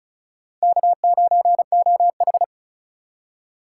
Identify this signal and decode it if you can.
Morse code, heard as a keyed tone, about 34 wpm, 700 Hz. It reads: K9OH
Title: USA Call Signs - 4 Characters - 4x Character Spacing - 35wpm